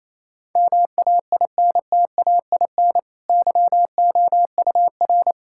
Morse code, heard as a keyed tone, about 28 wpm, 700 Hz. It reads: MAINTAIN YOUR
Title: Sentences from Top 1000 Words - Review Part 5 - 28wpm